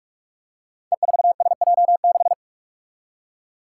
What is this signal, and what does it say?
Morse code, heard as a keyed tone, about 46 wpm, 700 Hz. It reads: E4SJ6
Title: Five Challenging Letter-Number Combinations 45wpm